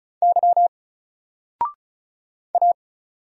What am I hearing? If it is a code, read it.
Morse code, heard as a keyed tone, about 36 wpm, 700 Hz. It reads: Y A
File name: Single-Letter-Number-35wpm